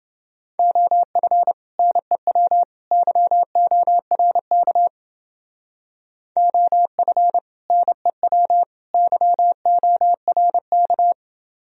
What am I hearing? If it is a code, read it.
Morse code, heard as a keyed tone, about 28 wpm, 700 Hz. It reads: OF NEW YORK OF NEW YORK